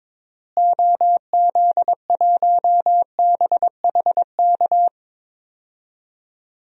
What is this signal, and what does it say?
Morse code, heard as a keyed tone, about 22 wpm, 700 Hz. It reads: OZ1BHK
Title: DXCC-100 Call Signs - 6 Characters - Repeated 3x - 22wpm